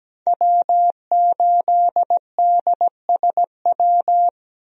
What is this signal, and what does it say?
Morse code, heard as a keyed tone, about 17 wpm, 700 Hz. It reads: W8DSW
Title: USA Call Signs - 5 Characters - 17wpm